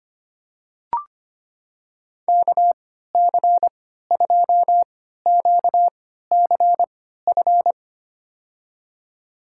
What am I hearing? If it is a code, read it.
Morse code, heard as a keyed tone, about 25 wpm, spaced as for 14 wpm, 700 Hz. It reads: KC2QCF
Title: USA Call Signs - 3x Character Spacing - 25wpm